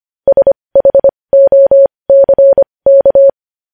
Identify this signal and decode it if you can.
Morse code, heard as a keyed tone, about 25 wpm, spaced as for 20 wpm, 550 Hz. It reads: SHOCK
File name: podcast / News headlines-1x-25x20